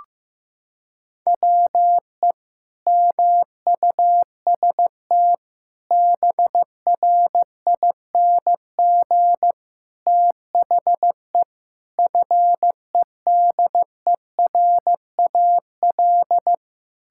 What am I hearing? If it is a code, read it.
Morse code, heard as a keyed tone, about 15 wpm, 700 Hz. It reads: WE MUST BRING THE FEDERAL
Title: Sentences from Top 1000 Words - Review Part 6 - 15wpm